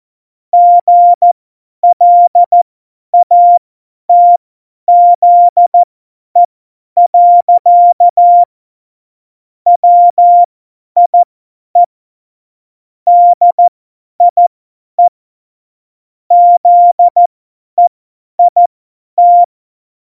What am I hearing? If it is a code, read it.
Morse code, heard as a keyed tone, about 14 wpm, spaced as for 10 wpm, 700 Hz. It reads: GLATZE. WIE DIE ZEIT
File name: Morsekurs / Witze / Witz 10.Juli 2018, 50 F70